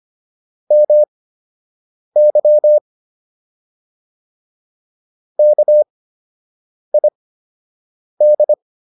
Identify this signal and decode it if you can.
Morse code, heard as a keyed tone, about 25 wpm, spaced as for 7 wpm, 600 Hz. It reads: MY KID